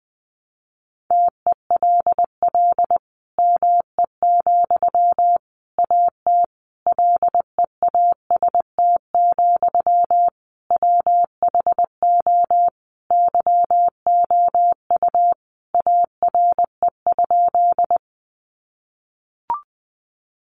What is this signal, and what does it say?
Morse code, heard as a keyed tone, about 20 wpm, 700 Hz. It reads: TELL ME, AT LEAST, WHO YOU ARE?
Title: Sentences from Top 700 Words 20wpm